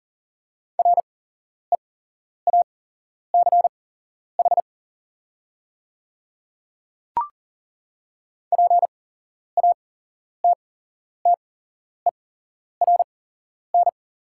Morse code, heard as a keyed tone, about 40 wpm, spaced as for 11 wpm, 690 Hz. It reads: REACH PATTERN
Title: 8x Character Spacing - Top 500 Words - 40wpm